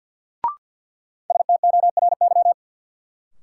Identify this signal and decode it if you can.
Morse code, heard as a keyed tone, about 52 wpm, 700 Hz. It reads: STORY
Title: Sets of 3 Words - New Word Drill - 50wpm